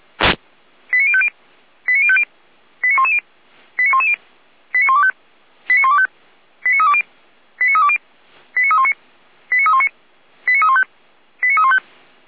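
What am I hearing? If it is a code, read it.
Five-tone selective call (ZVEI), heard as quick runs of five tones, tones of about 70 ms. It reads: 89050 89050 89110 89110 89125 89125 89320 89320 89319 89319 89315 89315
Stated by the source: ZVEI five-tone selective call 89140 received on SDR, Feuerwehr Orlamünde fire brigade alerting.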